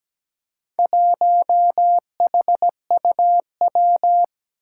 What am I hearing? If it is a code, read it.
Morse code, heard as a keyed tone, about 17 wpm, 700 Hz. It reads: 1HUW
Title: Four Challenging Letter-Number Combinations 17wpm